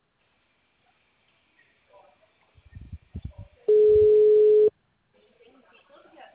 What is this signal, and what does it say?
Ringback tone (Europe, 425 Hz), with speech.
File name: out- -2016- - - .93543